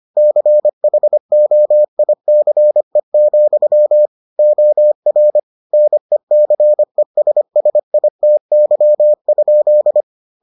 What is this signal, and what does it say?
Morse code, heard as a keyed tone, about 25 wpm, 600 Hz. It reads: CHOICE, OR NECESSITY?